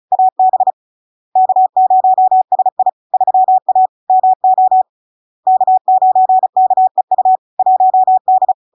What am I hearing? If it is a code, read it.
Morse code, heard as a keyed tone, about 36 wpm, 750 Hz. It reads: AB K0SI 3A MO K9KEU 1D
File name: LARK-FD-2022-Log-in-CW-35wpm-750-Hz